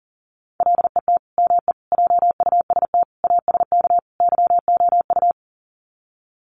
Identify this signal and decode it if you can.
Morse code, heard as a keyed tone, about 40 wpm, 700 Hz. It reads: LET ME JUST ASK YOU
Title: Sets of 5 Words Encore2 40wpm